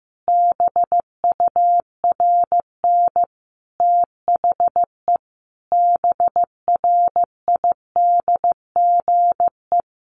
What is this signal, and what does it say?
Morse code, heard as a keyed tone, about 15 wpm, 700 Hz. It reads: BURN THE BRIDGE
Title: Sentences from Top 1300 Words 15wpm